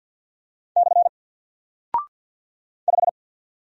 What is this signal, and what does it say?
Morse code, heard as a keyed tone, about 49 wpm, 700 Hz. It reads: / 5
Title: Single Character (Letter:Number:Punctuation:Prosign) 50wpm